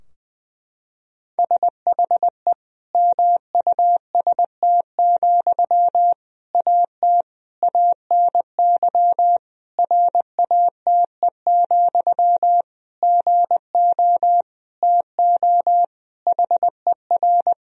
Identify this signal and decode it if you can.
Morse code, heard as a keyed tone, about 20 wpm, 700 Hz. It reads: SHE MUST, AT ANY RATE, GO TO HER
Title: Sentences from Top 1000 Words - Review Part 4 - 20wpm